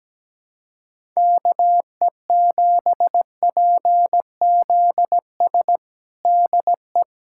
Morse code, heard as a keyed tone, about 17 wpm, 700 Hz. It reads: KE7PZS DE
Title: USA Callsign DE Callsign 17wpm